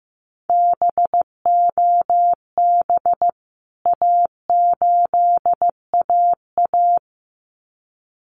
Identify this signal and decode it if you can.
Morse code, heard as a keyed tone, about 15 wpm, 700 Hz. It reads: BOB A8AA
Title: CWT Exchange 15wpm